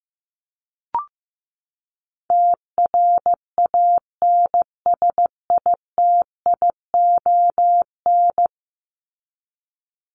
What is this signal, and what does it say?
Morse code, heard as a keyed tone, about 15 wpm, 700 Hz. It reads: TRANSITION
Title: Top 1751-2000 Words 15wpm